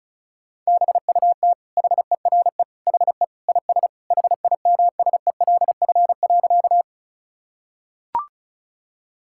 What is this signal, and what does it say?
Morse code, heard as a keyed tone, about 34 wpm, 700 Hz. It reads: BUT HERE HE IS HIMSELF.
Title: Sentences from Top 1600 Words 35wpm